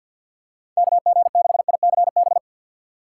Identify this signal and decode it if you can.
Morse code, heard as a keyed tone, about 52 wpm, 700 Hz. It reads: KG6ICB
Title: USA Call Signs - 6 Characters - 3x Character Spacing - 50wpm